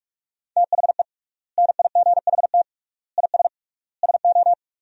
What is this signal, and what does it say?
Morse code, heard as a keyed tone, about 47 wpm, 700 Hz. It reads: THE NIGHT IS SO
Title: 3x Word-Spacing - Sentences from Top 500 Words - 45wpm